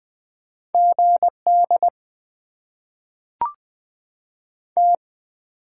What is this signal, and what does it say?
Morse code, heard as a keyed tone, about 20 wpm, 700 Hz. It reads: GD T
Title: Top 90 QSO Elements - 8x Character Spacing - 20wpm